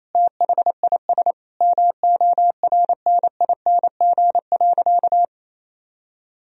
Morse code, heard as a keyed tone, about 28 wpm, 700 Hz. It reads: THIS MORNING.
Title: Sentences from Top 1000 Words - Review Part 3 - 28wpm